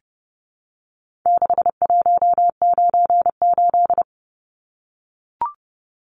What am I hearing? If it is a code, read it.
Morse code, heard as a keyed tone, about 30 wpm, 700 Hz. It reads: 6198
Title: Numbers - 4-Digit 30wpm